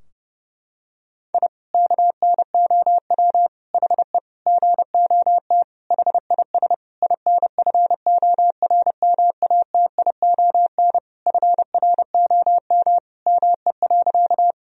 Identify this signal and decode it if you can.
Morse code, heard as a keyed tone, about 29 wpm, 700 Hz. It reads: I KNOW HE GOT HIS INFORMATION FROM ME.
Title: Sentences from Top 1000 Words - Review Part 3 - 30wpm